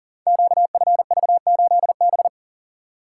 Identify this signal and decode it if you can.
Morse code, heard as a keyed tone, about 39 wpm, 700 Hz. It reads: QFV8B
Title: Code Groups - 5 Letter:Numbers - 40wpm